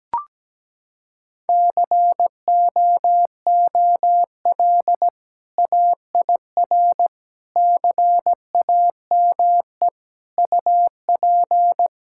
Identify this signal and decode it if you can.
Morse code, heard as a keyed tone, about 17 wpm, 700 Hz. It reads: COOL AIR CAME UP
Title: Sentences from Top 1600 Words 17wpm